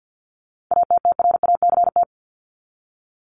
Morse code, heard as a keyed tone, about 50 wpm, 700 Hz. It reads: ATTRACT
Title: Top 1501-1750 Words 50wpm